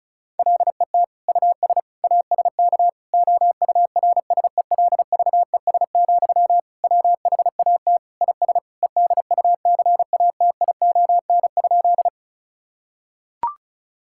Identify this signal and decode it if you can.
Morse code, heard as a keyed tone, about 36 wpm, 700 Hz. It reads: LET US ASK OURSELVES, WHAT IS EDUCATION?
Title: Sentences from Top 1600 Words 35wpm